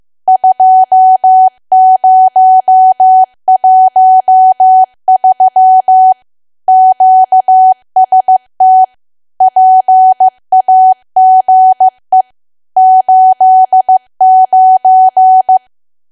Morse code, heard as a keyed tone, about 15 wpm, 750 Hz. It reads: 2013 QST PAGE 89
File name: cwa / W1AW-Level2 / 15WPM